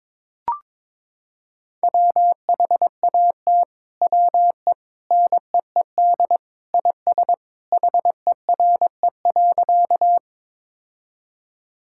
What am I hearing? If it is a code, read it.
Morse code, heard as a keyed tone, about 22 wpm, 700 Hz. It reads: WHAT WE NEED IS HERE.
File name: practice / 22wpm / Sentences with 2-Word Phrases - 22wpm